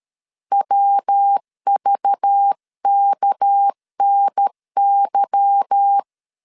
Morse code, heard as a keyed tone, about 13 wpm, 800 Hz. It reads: WVKNY
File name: LEZIONI / 8 YJ45 / 8 Y 12wpm 2228